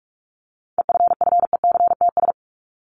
Morse code, heard as a keyed tone, about 44 wpm, 700 Hz. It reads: EFFECTS
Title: ES:S Suffix - Top 100 Words with Base Words 45wpm